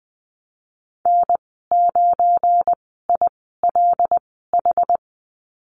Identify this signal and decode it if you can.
Morse code, heard as a keyed tone, about 19 wpm, spaced as for 14 wpm, 700 Hz. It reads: N9ILH
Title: 186 Lesson 30g - Letter b - Call Signs 20wpm